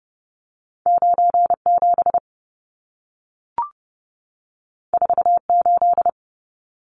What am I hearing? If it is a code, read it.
Morse code, heard as a keyed tone, about 30 wpm, 700 Hz. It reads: 97 48
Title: Numbers - 2-Digit 30wpm